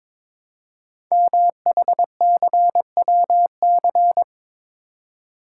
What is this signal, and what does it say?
Morse code, heard as a keyed tone, about 23 wpm, 700 Hz. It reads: MHCWC
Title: Code Groups - 5 Letter:Numbers - 22wpm